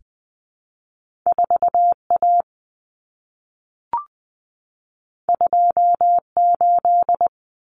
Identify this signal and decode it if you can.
Morse code, heard as a keyed tone, about 20 wpm, 700 Hz. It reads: 4A 28
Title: Common Cut Numbers - 2-Digit 20wpm